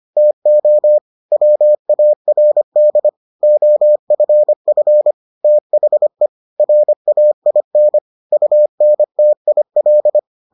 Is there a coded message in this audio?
Yes: Morse code from 0.2 to 10.2 s, heard as a keyed tone, about 25 wpm, 600 Hz. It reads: TO WARD OFF THE RAIN UNTIL